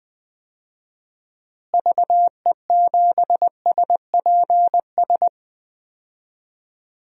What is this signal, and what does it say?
Morse code, heard as a keyed tone, about 20 wpm, 700 Hz. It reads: VE7SPS
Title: DXCC-100 Call Signs - 6 Characters - Repeated 2x - 20wpm